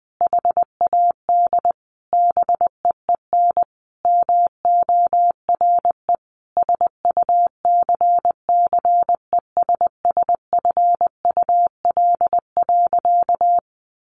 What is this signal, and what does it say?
Morse code, heard as a keyed tone, about 20 wpm, 700 Hz. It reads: HAD BEEN MORE SUCCESSFUL.
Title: Sentences from Top 1500 Words 20wpm